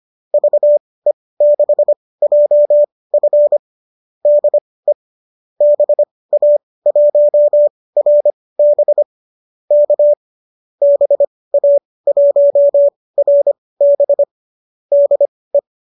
Morse code, heard as a keyed tone, about 25 wpm, spaced as for 18 wpm, 570 Hz. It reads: VE6JF DE BA1RB K BA1RB DE
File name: qso203 18